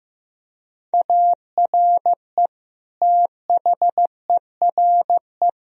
Morse code, heard as a keyed tone, about 15 wpm, 700 Hz. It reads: ARE THERE
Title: Sentences from Top 1000 Words - Review Part 4 - 15wpm